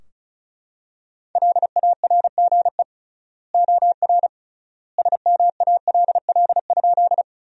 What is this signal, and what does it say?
Morse code, heard as a keyed tone, about 36 wpm, 700 Hz. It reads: LARGE OR SMALL?
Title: 3x Word-Spacing - Sentences from Top 200 Words - 35wpm